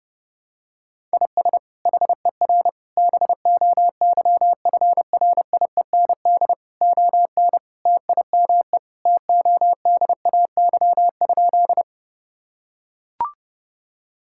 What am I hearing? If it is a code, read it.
Morse code, heard as a keyed tone, about 30 wpm, 700 Hz. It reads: IS HER BOYFRIEND ON TIME TODAY?
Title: Sentences from Top 1000 Words - Review Part 5 - 30wpm